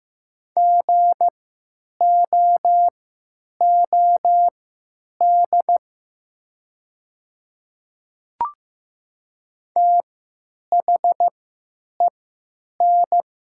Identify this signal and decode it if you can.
Morse code, heard as a keyed tone, about 15 wpm, spaced as for 9 wpm, 700 Hz. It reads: GOOD THEN
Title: 3x Character Spacing - Top 200 Words - 15wpm